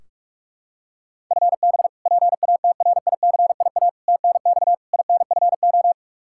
Morse code, heard as a keyed tone, about 46 wpm, 700 Hz. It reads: FB PATRICIA TNX INFO
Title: Practice QSO - Exchange 3 - More Info - 45wpm